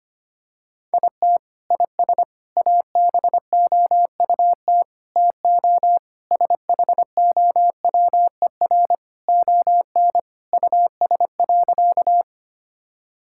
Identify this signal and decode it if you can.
Morse code, heard as a keyed tone, about 25 wpm, 700 Hz. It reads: IT IS ABOUT TO SHOWER ON US.
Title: Sentences from Top 1600 Words 25wpm